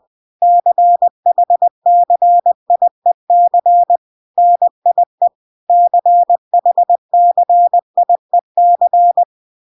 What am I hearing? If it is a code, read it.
Morse code, heard as a keyed tone, about 20 wpm, 700 Hz. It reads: CHCIEC NIE CHCIEC